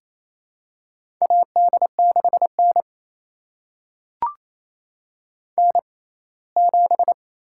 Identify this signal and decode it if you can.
Morse code, heard as a keyed tone, about 28 wpm, spaced as for 19 wpm, 700 Hz. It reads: AD6N N7
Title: USA Call Signs - 4 Characters - 6x Character Spacing - 28wpm